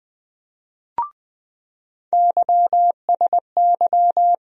Morse code, heard as a keyed tone, about 20 wpm, 700 Hz. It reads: YSY
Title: USA Call Sign Postfixes - 3 Letters 20wpm